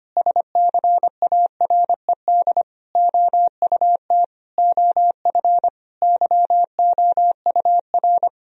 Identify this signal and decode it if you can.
Morse code, heard as a keyed tone, about 25 wpm, 700 Hz. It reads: SCARED OUT OF YOUR
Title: Sentences from Top 1000 Words - Review Part 4 - 25wpm